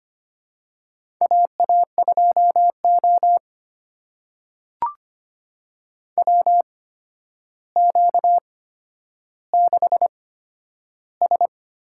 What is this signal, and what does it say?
Morse code, heard as a keyed tone, about 25 wpm, spaced as for 11 wpm, 700 Hz. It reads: AA2O WQ6S
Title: USA Call Signs - 4 Characters - 8x Character Spacing - 25wpm